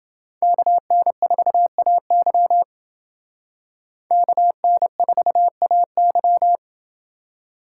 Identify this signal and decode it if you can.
Morse code, heard as a keyed tone, about 28 wpm, 700 Hz. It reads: KN4AY KN4AY